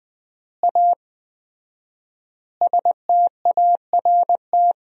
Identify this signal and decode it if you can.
Morse code, heard as a keyed tone, about 20 wpm, 700 Hz. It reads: A START
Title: 4x Word-Spacing - Sentences from Top 300 Words - 20wpm